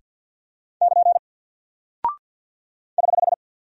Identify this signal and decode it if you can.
Morse code, heard as a keyed tone, about 48 wpm, 700 Hz. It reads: <KN> <HH>
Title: Single Character (Letter:Number:Punctuation:Prosign) Alt 50wpm